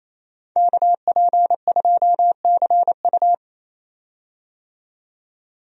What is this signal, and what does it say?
Morse code, heard as a keyed tone, about 28 wpm, 700 Hz. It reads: KP2CU
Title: DXCC-100 Call Signs - 5 Characters - 28wpm